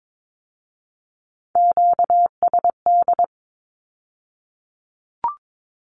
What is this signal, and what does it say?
Morse code, heard as a keyed tone, about 22 wpm, 700 Hz. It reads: QSD
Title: Q-Codes with Spoken Meaning - 22wpm